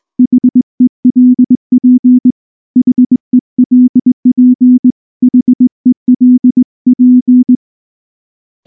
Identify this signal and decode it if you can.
Morse code, heard as a keyed tone, about 19 wpm, 250 Hz. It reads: HELP HELP HELP
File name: morse / morse-training-20WPM-3